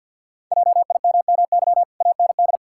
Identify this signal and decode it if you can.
Morse code, heard as a keyed tone, about 52 wpm, 700 Hz. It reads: JIMMY AND
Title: Pangram Challenge v2 - 50wpm